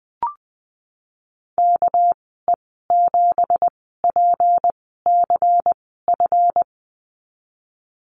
Morse code, heard as a keyed tone, about 20 wpm, spaced as for 15 wpm, 700 Hz. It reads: KE7PCF
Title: 219 Lesson 35g - Number 0 - Call Signs 20wpm